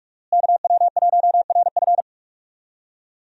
Morse code, heard as a keyed tone, about 46 wpm, 700 Hz. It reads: KW1RF
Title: USA Call Signs - 5 Characters - Repeated 2x - 45wpm